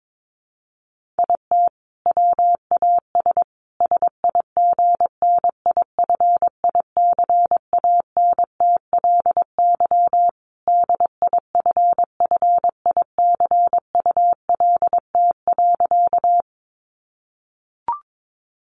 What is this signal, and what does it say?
Morse code, heard as a keyed tone, about 22 wpm, 700 Hz. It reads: IT WAS SIGNIFICANTLY DIFFICULT.
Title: Sentences from Top 1500 Words 22wpm